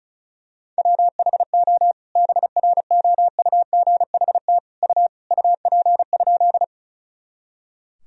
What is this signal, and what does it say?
Morse code, heard as a keyed tone, about 36 wpm, 700 Hz. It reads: WHO BROUGHT U UP?
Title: Sentences from Top 500 Words and Abbreviations 35wpm